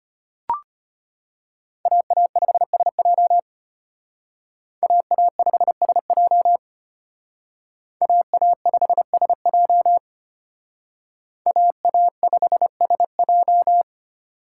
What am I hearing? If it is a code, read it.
Morse code, heard as a keyed tone, about 32 wpm, 700 Hz. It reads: AA5SJ AA5SJ AA5SJ AA5SJ